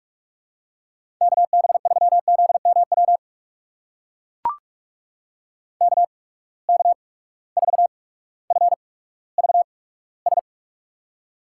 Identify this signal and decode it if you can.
Morse code, heard as a keyed tone, about 44 wpm, spaced as for 19 wpm, 700 Hz. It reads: KB3ZMW KK4FVS
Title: USA Call Signs - 6 Characters - 8x Character Spacing - 45wpm